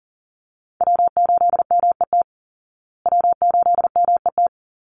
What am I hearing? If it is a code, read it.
Morse code, heard as a keyed tone, about 40 wpm, 700 Hz. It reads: W8MET W8MET
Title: Call Signs Repeated 2x - 40wpm